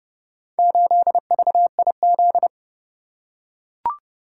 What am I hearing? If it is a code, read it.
Morse code, heard as a keyed tone, about 30 wpm, 700 Hz. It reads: 8VIZ